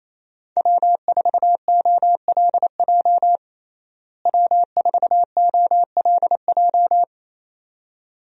Morse code, heard as a keyed tone, about 28 wpm, 700 Hz. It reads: W4OLJ W4OLJ
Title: USA Call Signs - 5 Characters - Repeated 2x - 28wpm